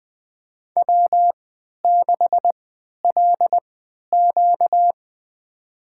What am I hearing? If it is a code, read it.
Morse code, heard as a keyed tone, about 20 wpm, spaced as for 11 wpm, 700 Hz. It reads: W6LQ
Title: USA Call Signs - 4 Characters - 3x Character Spacing - 20wpm